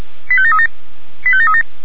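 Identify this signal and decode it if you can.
Five-tone selective call (ZVEI), heard as quick runs of five tones, tones of about 80 ms. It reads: 85737 85737
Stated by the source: Five-tone ZVEI selective call, received by SDR.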